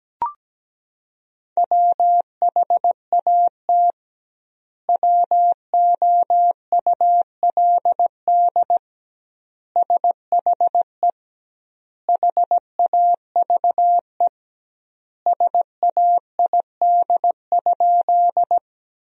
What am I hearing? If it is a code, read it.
Morse code, heard as a keyed tone, about 17 wpm, 700 Hz. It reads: WHAT WOULD SHE HAVE SAID?
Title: 2x Word-Spacing - Sentences from Top 100 Words - 17wpm